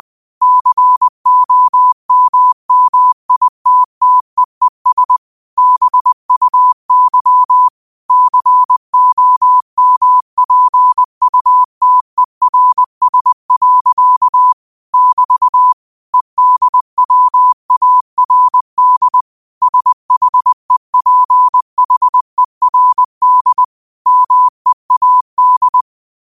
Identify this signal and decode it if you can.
Morse code, heard as a keyed tone, about 20 wpm, 1000 Hz. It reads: COMMITTEES BUY COMPUTERS. <BT> EDWARD SHEPHERD MEAD
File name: qotdfiles / 20wpm